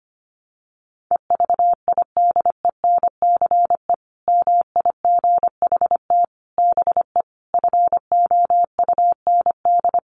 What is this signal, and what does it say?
Morse code, heard as a keyed tone, about 25 wpm, 700 Hz. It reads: EVIDENCE MIGHT BE FOUND
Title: Sentences from Top 1300 Words 25wpm